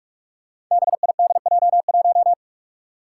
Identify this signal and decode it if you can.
Morse code, heard as a keyed tone, about 46 wpm, 700 Hz. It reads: BIDJ1
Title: Five Challenging Letter-Number Combinations 45wpm